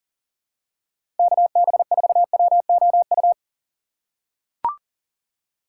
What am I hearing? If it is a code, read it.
Morse code, heard as a keyed tone, about 37 wpm, 700 Hz. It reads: KB4WOU
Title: USA Call Signs - 6 Characters - 4x Character Spacing - 40wpm